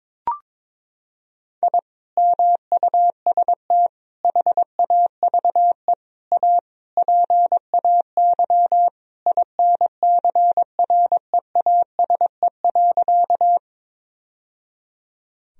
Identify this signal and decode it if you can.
Morse code, heard as a keyed tone, about 22 wpm, 700 Hz. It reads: I MUST HAVE A PAY INCREASE.
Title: Sentences from Top 1000 Words - Review Part 4 - 22wpm